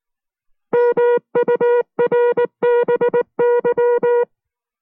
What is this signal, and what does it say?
Morse code, heard as a keyed tone, about 19 wpm, 480 Hz. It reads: MURBY